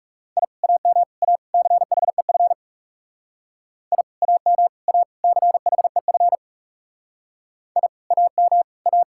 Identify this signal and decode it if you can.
Morse code, heard as a keyed tone, about 36 wpm, 700 Hz. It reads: I AM A CHEF I AM A CHEF I AM A
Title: Speed-Racing Common QSO Phrases Encore - 30wpm